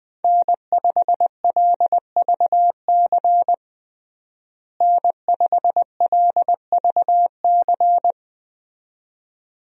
Morse code, heard as a keyed tone, about 21 wpm, 700 Hz. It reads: N5LVC N5LVC
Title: USA Call Signs - 5 Characters - Repeated 2x - 20wpm